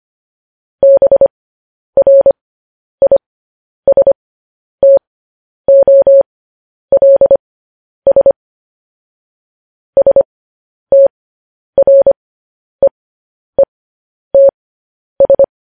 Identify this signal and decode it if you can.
Morse code, heard as a keyed tone, about 25 wpm, spaced as for 10 wpm, 550 Hz. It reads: BRISTOLS STREETS
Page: news-headlines-in-morse-code-at-10-wpm